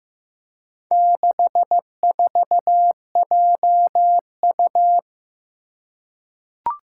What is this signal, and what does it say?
Morse code, heard as a keyed tone, about 15 wpm, 700 Hz. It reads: 64JU
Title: Four Challenging Letter-Number Combinations 15wpm